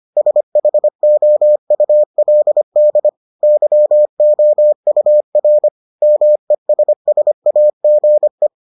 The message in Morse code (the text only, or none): SHOULD YOUR MESSAGE